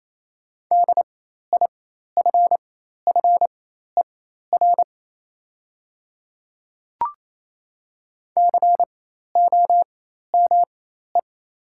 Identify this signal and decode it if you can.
Morse code, heard as a keyed tone, about 28 wpm, spaced as for 13 wpm, 690 Hz. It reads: DIFFER COME
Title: 4x Character Spacing - Top 200 Words - 28wpm